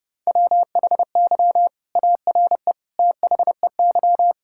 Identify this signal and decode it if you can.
Morse code, heard as a keyed tone, about 30 wpm, 700 Hz. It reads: WHY ARE THEY
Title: Sentences from Top 1000 Words - Review Part 5 - 30wpm